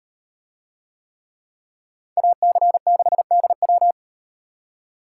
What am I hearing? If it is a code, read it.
Morse code, heard as a keyed tone, about 38 wpm, 700 Hz. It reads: AC6DW